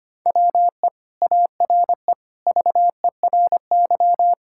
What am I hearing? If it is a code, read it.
Morse code, heard as a keyed tone, about 25 wpm, 700 Hz. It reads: WE ARE VERY
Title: Sentences from Top 1000 Words - Review Part 6 - 25wpm